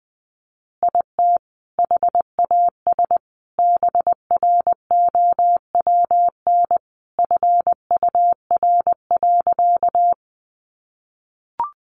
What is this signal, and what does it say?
Morse code, heard as a keyed tone, about 20 wpm, 700 Hz. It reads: IT HAS BROWN FUR.
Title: Easy Phrases 20wpm